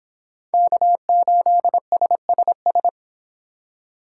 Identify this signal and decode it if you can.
Morse code, heard as a keyed tone, about 25 wpm, 700 Hz. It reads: K8SSS